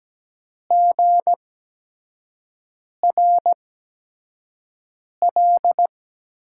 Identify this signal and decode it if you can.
Morse code, heard as a keyed tone, about 17 wpm, spaced as for 5 wpm, 700 Hz. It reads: GRL
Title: USA Call Signs - 5 Characters - 8x Character Spacing - 17wpm